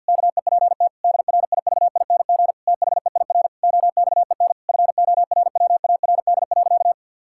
Morse code, heard as a keyed tone, about 50 wpm, 700 Hz. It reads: KEPT DRIVING THEIR OXEN FORWARD.
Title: Pangram Challenge v2 - 50wpm